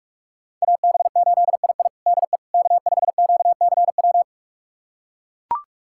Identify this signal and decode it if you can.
Morse code, heard as a keyed tone, about 42 wpm, 700 Hz. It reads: AB8II DE K5QCW
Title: USA Callsign DE Callsign - 5 Characters - 45wpm